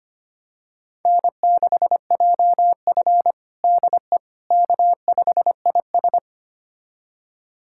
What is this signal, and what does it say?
Morse code, heard as a keyed tone, about 25 wpm, 700 Hz. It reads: N6JF DE K5IS
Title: USA Callsign DE Callsign - 4 Characters - 25wpm